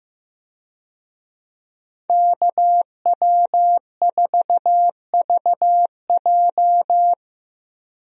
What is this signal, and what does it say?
Morse code, heard as a keyed tone, about 15 wpm, 700 Hz. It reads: KW4VJ
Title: USA Call Signs - 5 Characters - Repeated 3x - 15wpm